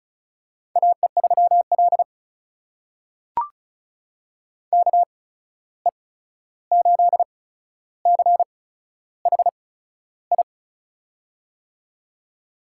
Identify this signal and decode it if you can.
Morse code, heard as a keyed tone, about 34 wpm, spaced as for 13 wpm, 700 Hz. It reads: AE3L KE8CHI
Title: USA Call Signs - 8x Character Spacing - 35wpm